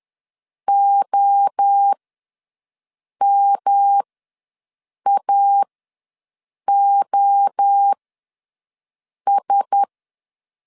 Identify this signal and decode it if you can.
Morse code, heard as a keyed tone, about 10 wpm, spaced as for 5 wpm, 800 Hz. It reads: OMAOS
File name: LEZIONI / 5 LBVC / 5 LBV ws 10wpm 3645